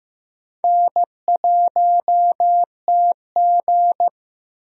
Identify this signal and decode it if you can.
Morse code, heard as a keyed tone, about 15 wpm, 700 Hz. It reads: N1TG
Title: USA Call Signs - 4 Characters - Repeated 3x - 15wpm